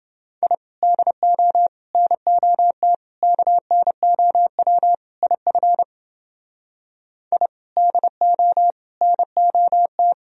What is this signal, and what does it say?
Morse code, heard as a keyed tone, about 29 wpm, 700 Hz. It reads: I DO NOT KNOW IF I DO NOT